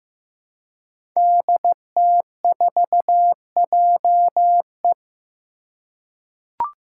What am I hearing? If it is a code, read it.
Morse code, heard as a keyed tone, about 15 wpm, 700 Hz. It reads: DT4JE
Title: Five Challenging Letter-Number Combinations 15wpm